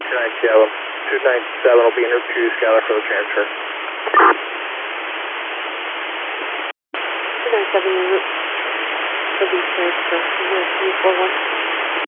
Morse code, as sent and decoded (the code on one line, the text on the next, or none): none